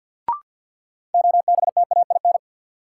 Morse code, heard as a keyed tone, about 53 wpm, 700 Hz. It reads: OBTAIN
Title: Sentences from Top 1300 Words - New Word Drill - 50wpm